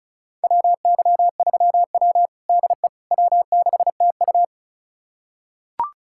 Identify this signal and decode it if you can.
Morse code, heard as a keyed tone, about 36 wpm, 700 Hz. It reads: WY3W DE W6TU
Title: USA Callsign DE Callsign - 4 Characters - 35wpm